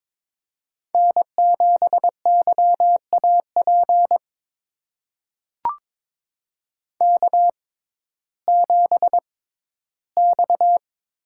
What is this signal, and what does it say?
Morse code, heard as a keyed tone, about 22 wpm, spaced as for 14 wpm, 700 Hz. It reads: N7YAP K7X
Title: USA Call Signs - 6x Character Spacing - 22wpm